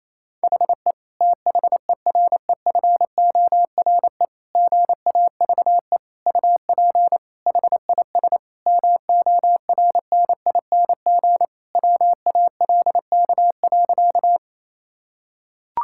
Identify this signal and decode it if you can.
Morse code, heard as a keyed tone, about 28 wpm, 700 Hz. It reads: HE THEREFORE GAVE UP HIS MORNING WALK.
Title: Sentences from Top 1000 Words - Review Part 5 - 28wpm